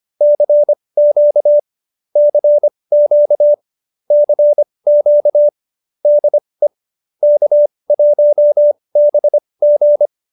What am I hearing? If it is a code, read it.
Morse code, heard as a keyed tone, about 25 wpm, spaced as for 20 wpm, 590 Hz. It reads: CQ CQ CQ DE K1BG